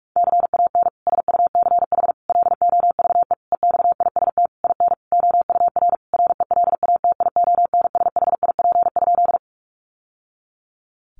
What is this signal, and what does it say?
Morse code, heard as a keyed tone, about 44 wpm, 700 Hz. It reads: CAN SUCH LOVE EXIST IN OUR RELATIONSHIP?
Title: Sentences from Top 1100 Words 45wpm